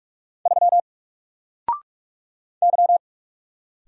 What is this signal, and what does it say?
Morse code, heard as a keyed tone, about 46 wpm, 700 Hz. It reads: 3 Y
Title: Single Letter-Number Alt 45wpm